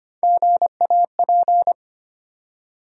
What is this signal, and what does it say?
Morse code, heard as a keyed tone, about 25 wpm, 700 Hz. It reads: GAP
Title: Three Letter Words - 25wpm